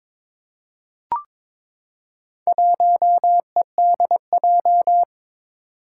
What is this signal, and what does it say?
Morse code, heard as a keyed tone, about 22 wpm, 700 Hz. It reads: 1EDJ
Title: Four Challenging Letter-Number Combinations 22wpm